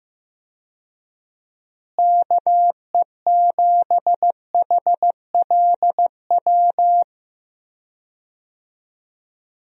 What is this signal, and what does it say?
Morse code, heard as a keyed tone, about 15 wpm, 700 Hz. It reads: KE7HLW
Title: USA Call Signs - 6 Characters - Repeated 2x - 15wpm